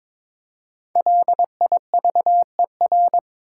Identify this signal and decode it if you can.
Morse code, heard as a keyed tone, about 22 wpm, 700 Hz. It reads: LIVER
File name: 5 Letter Words - First 3 Letters Given - 22wpm